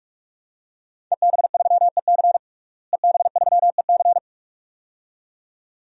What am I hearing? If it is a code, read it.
Morse code, heard as a keyed tone, about 46 wpm, 700 Hz. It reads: EB3EC EB3EC
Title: DXCC-100 Call Signs - 5 Characters - Repeated 2x - 45wpm